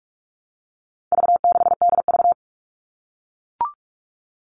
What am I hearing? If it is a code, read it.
Morse code, heard as a keyed tone, about 44 wpm, 700 Hz. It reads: V6DV
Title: All Cut Numbers - 4-Digit 45wpm